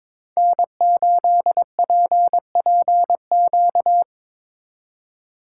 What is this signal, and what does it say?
Morse code, heard as a keyed tone, about 25 wpm, 700 Hz. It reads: N8PPQ
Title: Speed Racing - USA 1x3 Call Signs 22wpm